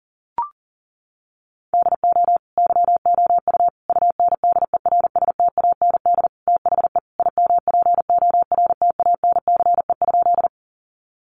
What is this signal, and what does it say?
Morse code, heard as a keyed tone, about 40 wpm, 700 Hz. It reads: DO YOU UNDERSTAND THE IMPORTANCE?
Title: Sentences from Top 1200 Words 40wpm